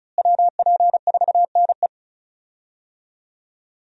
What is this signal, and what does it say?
Morse code, heard as a keyed tone, about 36 wpm, 700 Hz. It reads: WP4NE
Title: USA Call Signs - 5 Characters - Repeated 2x - 35wpm